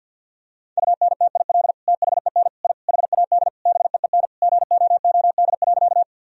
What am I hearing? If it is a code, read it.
Morse code, heard as a keyed tone, about 50 wpm, 700 Hz. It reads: UNTIL THEN I HAD BEEN GOOD.
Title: Sentences from Top 1000 Words - Review Part 1 - 50wpm